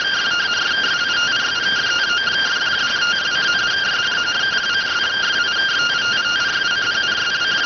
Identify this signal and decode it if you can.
Two-tone teleprinter signal, about 170 Hz shift.